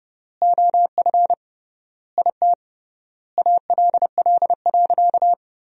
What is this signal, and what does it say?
Morse code, heard as a keyed tone, about 30 wpm, 700 Hz. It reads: OF IT ALL.
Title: 3x Word-Spacing - Sentences from Top 200 Words - 30wpm